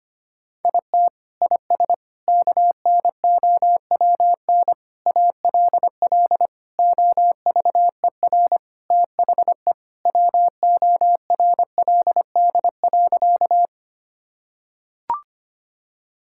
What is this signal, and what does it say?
Morse code, heard as a keyed tone, about 25 wpm, 700 Hz. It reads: IT IS KNOWN ALL OVER THE WORLD.
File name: practice / 25wpm / Sentences with 4-Word Phrases - 25wpm